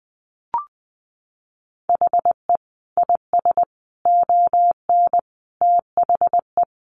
Morse code, heard as a keyed tone, about 19 wpm, 700 Hz. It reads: HE IS ON THE
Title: Sentences from Top 1000 Words 20wpm